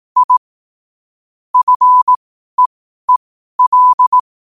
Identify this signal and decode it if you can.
Morse code, heard as a keyed tone, about 18 wpm, spaced as for 12 wpm, 1000 Hz. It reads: I FEEL
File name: qotdfiles / 12wpm